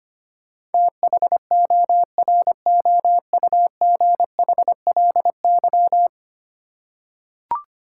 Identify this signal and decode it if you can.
Morse code, heard as a keyed tone, about 25 wpm, 700 Hz. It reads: THOROUGHLY
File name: practice / 25wpm / Ten Letter Words - 25wpm